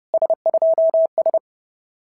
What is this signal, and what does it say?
Morse code, heard as a keyed tone, about 30 wpm, 650 Hz. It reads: S2S
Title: SOTA QSO Practice 30wpm